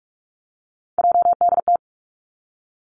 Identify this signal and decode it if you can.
Morse code, heard as a keyed tone, about 46 wpm, 700 Hz. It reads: JDT
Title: Three Challenging Letter-Number Combinations 45wpm